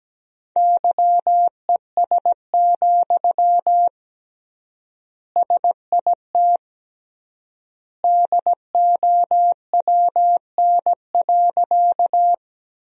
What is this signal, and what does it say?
Morse code, heard as a keyed tone, about 17 wpm, 700 Hz. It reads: YES, SIT DOWN.
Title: 3x Word-Spacing - Sentences from Top 500 Words - 17wpm